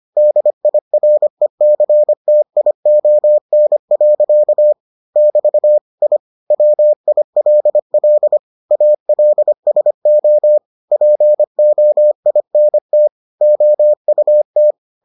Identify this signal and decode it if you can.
Morse code, heard as a keyed tone, about 25 wpm, 600 Hz. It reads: DIRECTION. <BT> I WILL ALSO POINT OUT